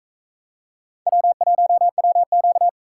Morse code, heard as a keyed tone, about 41 wpm, 700 Hz. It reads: W1WQ